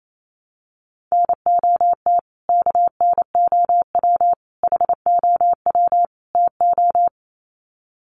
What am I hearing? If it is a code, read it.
Morse code, heard as a keyed tone, about 28 wpm, 700 Hz. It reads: NOT KNOW HOW TO
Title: Sets of 4 Words 28wpm